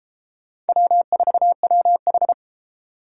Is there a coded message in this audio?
Yes: Morse code from 0.7 to 2.3 s, heard as a keyed tone, about 33 wpm, 700 Hz. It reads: W4WH